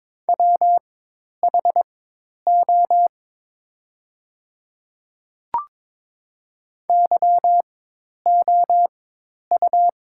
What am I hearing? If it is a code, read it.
Morse code, heard as a keyed tone, about 21 wpm, spaced as for 10 wpm, 700 Hz. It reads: WHO YOU
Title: 4x Character Spacing - Top 100 Words - 22wpm